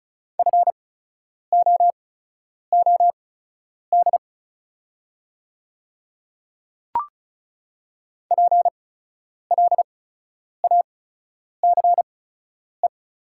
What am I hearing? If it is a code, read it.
Morse code, heard as a keyed tone, about 36 wpm, spaced as for 10 wpm, 690 Hz. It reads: FOOD PLACE
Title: 8x Character Spacing - Top 300 Words - 35wpm